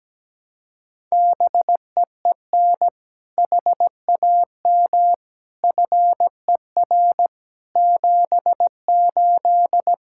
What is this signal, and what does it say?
Morse code, heard as a keyed tone, about 17 wpm, 700 Hz. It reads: BEEN HAM FER 78
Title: K1USN SST QSO Practice 17wpm